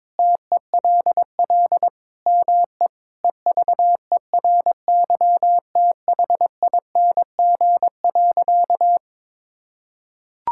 Morse code, heard as a keyed tone, about 22 wpm, 700 Hz. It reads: TELL ME EVERYTHING.
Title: Sentences from Top 1600 Words 22wpm